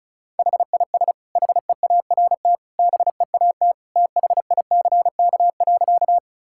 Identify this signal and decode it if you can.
Morse code, heard as a keyed tone, about 36 wpm, 700 Hz. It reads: HIS HEART BEAT THICK.
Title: Sentences from Top 1600 Words 35wpm